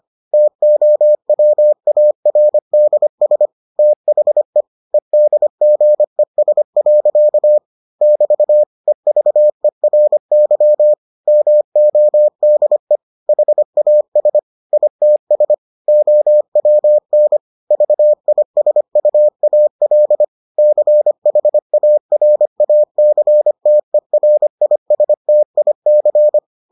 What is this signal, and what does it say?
Morse code, heard as a keyed tone, about 25 wpm, 600 Hz. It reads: TOWARDS THE EDGES. <BT> EVERY MODE HAS ITS OWN VISUAL CHARACTERISTIC